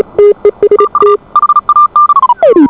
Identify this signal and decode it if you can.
Morse code, heard as a keyed tone, about 36 wpm, 1190 Hz. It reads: TU HAN